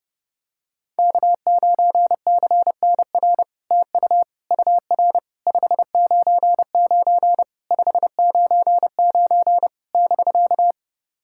Morse code, heard as a keyed tone, about 30 wpm, 700 Hz. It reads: K9CNR TU UR 599 599 <BK>
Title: SOTA QSO Practice 30wpm